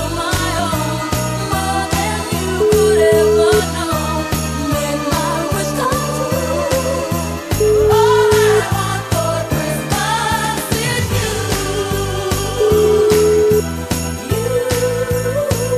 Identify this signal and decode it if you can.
Ringback tone (Europe, 425 Hz), with noise.